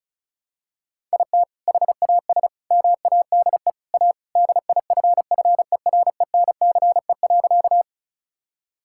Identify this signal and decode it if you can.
Morse code, heard as a keyed tone, about 36 wpm, 700 Hz. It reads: IT HAS MADE A DIFFERENCE.
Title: Sentences from Top 1000 Words - Review Part 5 - 35wpm